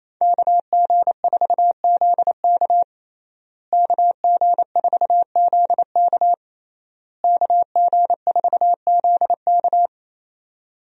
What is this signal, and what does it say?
Morse code, heard as a keyed tone, about 28 wpm, 700 Hz. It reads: KG4ZK KG4ZK KG4ZK
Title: USA Call Signs - 5 Characters - Repeated 3x - 28wpm